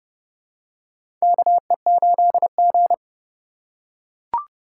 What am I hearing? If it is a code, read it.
Morse code, heard as a keyed tone, about 29 wpm, 700 Hz. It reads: KE8G
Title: USA Call Signs - 4 Characters - 2x Character Spacing - 30wpm